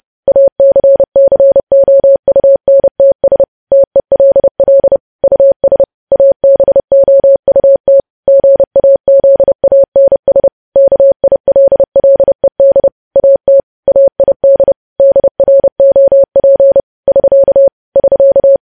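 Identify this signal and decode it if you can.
Morse code, heard as a keyed tone, about 30 wpm, 550 Hz. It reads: ACCOUNTS TELL US ABOUT GAZANS KILLED AT AID DROP <SK> <SK>
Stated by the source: Morse code practice of news headlines at 30 wpm, preceded by vvv.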